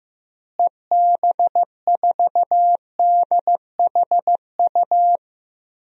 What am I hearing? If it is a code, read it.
Morse code, heard as a keyed tone, about 15 wpm, 700 Hz. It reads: EB4DHU
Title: DXCC-100 Call Signs - 6 Characters - Repeated 2x - 15wpm